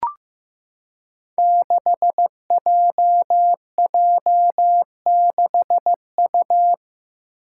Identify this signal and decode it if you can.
Morse code, heard as a keyed tone, about 15 wpm, 700 Hz. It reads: 6JJ6U
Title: Five Challenging Letter-Number Combinations 15wpm